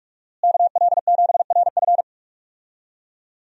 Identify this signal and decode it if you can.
Morse code, heard as a keyed tone, about 47 wpm, 700 Hz. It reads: KL7RF
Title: DXCC-100 Call Signs - 5 Characters - Repeated 2x - 45wpm